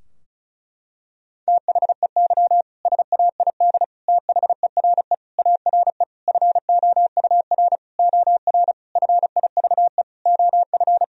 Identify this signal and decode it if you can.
Morse code, heard as a keyed tone, about 34 wpm, 700 Hz. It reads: THEY SAID THERE ARE FOUR OR FIVE OF
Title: Sentences from Top 1000 Words - Review Part 2 - 35wpm